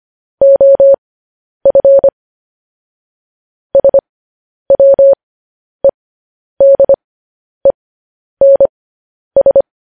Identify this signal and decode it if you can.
Morse code, heard as a keyed tone, about 25 wpm, spaced as for 10 wpm, 550 Hz. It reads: OF SWEDENS